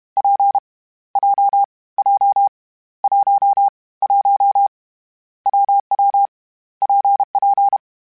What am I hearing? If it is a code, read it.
Morse code, heard as a keyed tone, about 32 wpm, 800 Hz. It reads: P J J 1 1 WW PP
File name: Variable Speed / Variable Speed-mp3s / WPJ WPJ1 sp @30